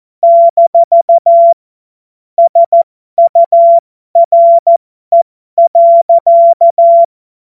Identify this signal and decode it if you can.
Morse code, heard as a keyed tone, about 14 wpm, spaced as for 12 wpm, 680 Hz. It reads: - SURE.
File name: Morsekurs / Witze